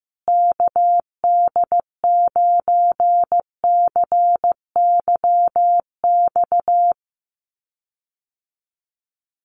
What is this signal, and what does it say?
Morse code, heard as a keyed tone, about 15 wpm, 700 Hz. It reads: KD9CYX
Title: USA Call Signs Repeated 3x 15wpm